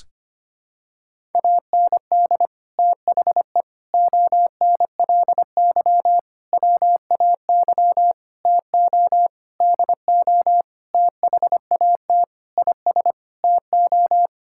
Morse code, heard as a keyed tone, about 25 wpm, 700 Hz. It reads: AND THE ONLY WAY TO DO THAT IS TO
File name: practice / 25wpm / Sentences with 9-Word Phrases - 25wpm